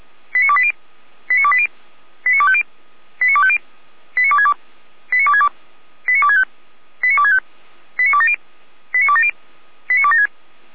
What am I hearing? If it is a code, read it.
Five-tone selective call (ZVEI), heard as quick runs of five tones, tones of about 70 ms. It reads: 89280 89280 89250 89250 89262 89262 89265 89265 89270 89270 89276